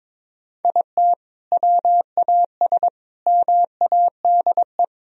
Morse code, heard as a keyed tone, about 22 wpm, 700 Hz. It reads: IT WAS MADE
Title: Sentences from Top 1000 Words - Review Part 4 - 22wpm